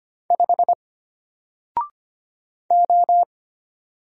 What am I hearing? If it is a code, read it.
Morse code, heard as a keyed tone, about 25 wpm, 700 Hz. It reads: 5 O
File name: Single-Letter-Number-25wpm